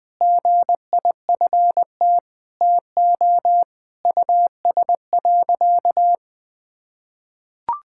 Morse code, heard as a keyed tone, about 21 wpm, 700 Hz. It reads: GIFT TO US.
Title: Sentences from Top 1000 Words - Review Part 6 - 20wpm